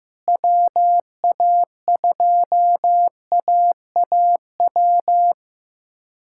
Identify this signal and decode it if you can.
Morse code, heard as a keyed tone, about 16 wpm, 700 Hz. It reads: WA2AAW
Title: K1USN SST QSO Practice - 15wpm